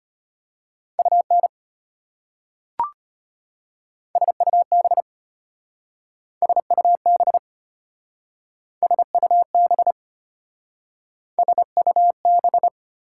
Morse code, heard as a keyed tone, about 33 wpm, 700 Hz. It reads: UN SUB SUB SUB SUB